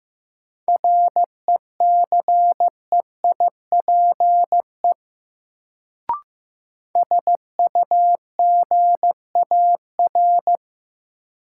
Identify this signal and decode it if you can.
Morse code, heard as a keyed tone, about 15 wpm, 700 Hz. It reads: RECEIPE SUGAR
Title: Sentences from Top 1200 Words - New Word Drill Alt - 15wpm